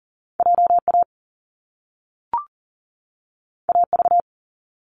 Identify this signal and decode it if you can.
Morse code, heard as a keyed tone, about 42 wpm, 700 Hz. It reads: JA AV
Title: Top Two Letter Combinations in Words 40wpm